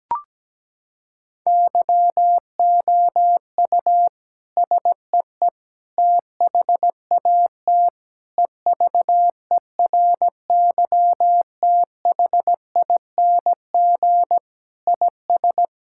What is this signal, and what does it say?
Morse code, heard as a keyed tone, about 17 wpm, 700 Hz. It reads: YOU SEE THAT EVERYTHING IS
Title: Sentences from Top 1000 Words - Review Part 6 - 17wpm